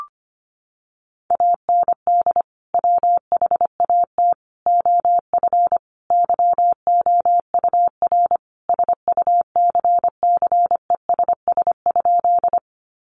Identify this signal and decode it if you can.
Morse code, heard as a keyed tone, about 25 wpm, 700 Hz. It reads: AND WHAT OF YOUR SUCCESS?
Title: Sentences from Top 900 Words 25wpm